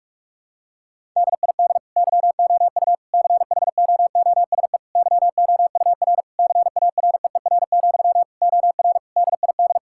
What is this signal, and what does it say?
Morse code, heard as a keyed tone, about 46 wpm, 700 Hz. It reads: DID YOU CHOOSE YOUR CAREER, OR DID
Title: Sentences from Top 1000 Words - Review Part 6 - 45wpm